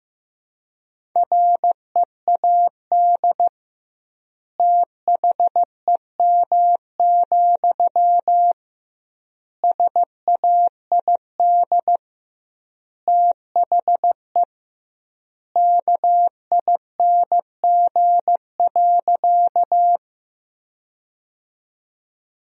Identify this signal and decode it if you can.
Morse code, heard as a keyed tone, about 16 wpm, 700 Hz. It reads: READ THEM, SAID THE KING.
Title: 2x Word-Spacing - Sentences from Top 400 Words - 15wpm